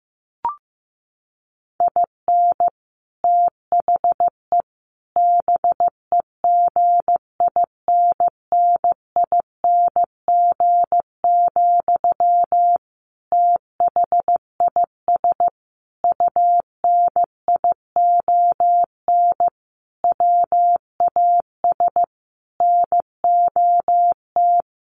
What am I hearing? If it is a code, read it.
Morse code, heard as a keyed tone, about 15 wpm, 700 Hz. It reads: IN THE BEGINNING, THIS UNION WAS NOT
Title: Sentences from Top 1400 Words 15wpm